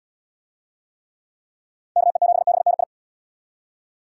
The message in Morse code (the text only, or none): KE7BNI